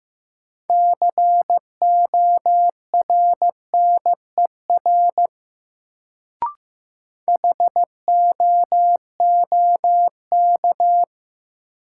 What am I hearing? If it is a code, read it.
Morse code, heard as a keyed tone, about 15 wpm, 700 Hz. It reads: CORNER HOOK
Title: Sentences from Top 1600 Words - New Word Drill Alt - 15wpm